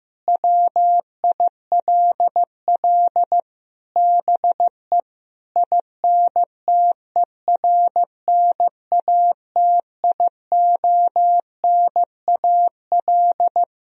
Morse code, heard as a keyed tone, about 15 wpm, 700 Hz. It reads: WILL BE INTERNATIONAL
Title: Sentences from Top 1000 Words - Review Part 4 - 15wpm